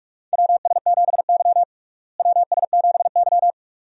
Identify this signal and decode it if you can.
Morse code, heard as a keyed tone, about 42 wpm, 700 Hz. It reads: WS7Y WS7Y
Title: USA Call Signs - 4 Characters - Repeated 2x - 45wpm